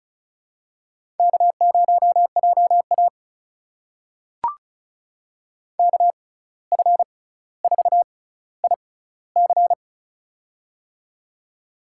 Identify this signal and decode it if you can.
Morse code, heard as a keyed tone, about 36 wpm, spaced as for 17 wpm, 700 Hz. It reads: K0JA KF4IC
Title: USA Call Signs - 6x Character Spacing - 35wpm